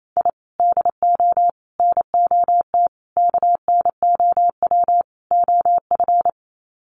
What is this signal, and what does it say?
Morse code, heard as a keyed tone, about 28 wpm, 700 Hz. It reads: I DO NOT KNOW OF
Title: Sets of 5 Words Encore2 28wpm